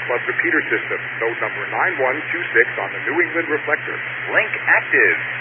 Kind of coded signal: Emergency Alert System (SAME) tones